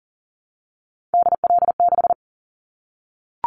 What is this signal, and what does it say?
Morse code, heard as a keyed tone, about 40 wpm, 700 Hz. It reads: DL6
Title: Callsign Prefixes for Top 25 DXCC - 40wpm